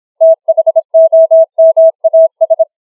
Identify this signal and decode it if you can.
Morse code, heard as a keyed tone, about 26 wpm, 650 Hz. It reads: THOMAS
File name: challenge-135